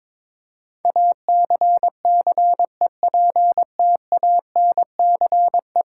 Morse code, heard as a keyed tone, about 22 wpm, 700 Hz. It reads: ACCEPTANCE
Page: morse-code-ninja-practice-22wpm